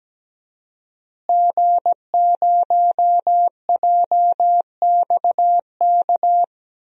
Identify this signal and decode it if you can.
Morse code, heard as a keyed tone, about 17 wpm, 700 Hz. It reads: G0JXK
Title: DXCC-100 Call Signs - 5 Characters - Repeated 2x - 17wpm